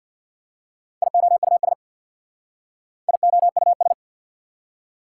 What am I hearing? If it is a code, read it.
Morse code, heard as a keyed tone, about 59 wpm, 690 Hz. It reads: IOUS IOUS
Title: Speed-Racing Prefixes and Suffixes 50wpm